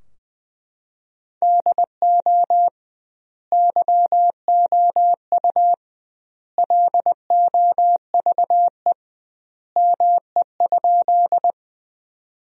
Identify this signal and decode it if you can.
Morse code, heard as a keyed tone, about 20 wpm, 700 Hz. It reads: DO YOU LOVE ME?
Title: 2x Word-Spacing - Sentences from Top 400 Words - 20wpm